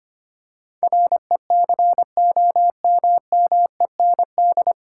Morse code, heard as a keyed tone, about 25 wpm, 700 Hz. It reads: RECOMMEND
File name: Nine Letter Words - 25wpm